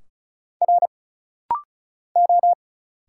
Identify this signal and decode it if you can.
Morse code, heard as a keyed tone, about 34 wpm, 700 Hz. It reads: R O
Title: Single Letters - Rapid-Fire 35wpm